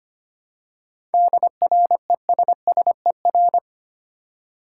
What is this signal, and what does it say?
Morse code, heard as a keyed tone, about 25 wpm, 690 Hz. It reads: DRESSER
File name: practice / 25wpm / Easy Words with Hints - 25wpm